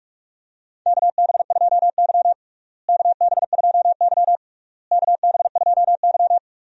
Morse code, heard as a keyed tone, about 46 wpm, 700 Hz. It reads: KB2Y KB2Y KB2Y
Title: USA Call Signs - 4 Characters - Repeated 3x - 45wpm